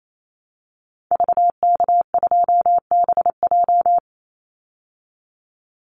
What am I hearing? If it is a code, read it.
Morse code, heard as a keyed tone, about 28 wpm, 700 Hz. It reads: VK2BJ
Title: Call Signs Repeated 2x - 28wpm